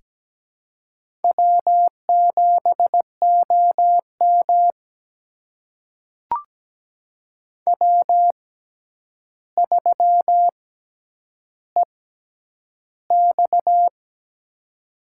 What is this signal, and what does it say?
Morse code, heard as a keyed tone, about 17 wpm, spaced as for 9 wpm, 700 Hz. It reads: W7OM W3EX
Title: USA Call Signs - 4 Characters - 6x Character Spacing - 17wpm